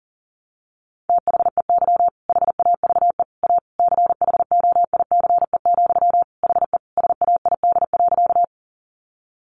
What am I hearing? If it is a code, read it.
Morse code, heard as a keyed tone, about 37 wpm, 700 Hz. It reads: THEY HAVE A CHOICE, HE SAID.
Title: Sentences from Top 900 Words 40wpm